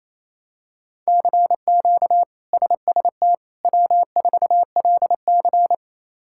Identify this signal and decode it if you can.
Morse code, heard as a keyed tone, about 28 wpm, 700 Hz. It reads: CQ SST W4LC
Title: K1USN SST QSO Practice - 28wpm